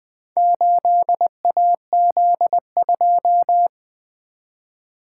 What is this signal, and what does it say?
Morse code, heard as a keyed tone, about 20 wpm, 700 Hz. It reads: 8AZ2
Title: Four Letter-Number Combinations - 20wpm